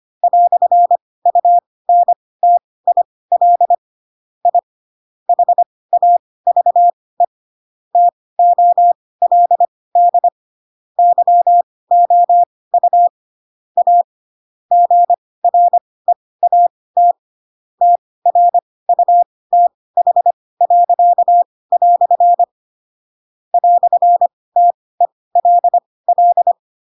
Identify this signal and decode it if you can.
Morse code, heard as a keyed tone, about 25 wpm, spaced as for 18 wpm, 700 Hz. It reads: "UNTIL I HAVE TOLD YOU A GREAT TRUTH." "TELL
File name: Pinocchio / Pinocchio ch4 12 25 ik1hge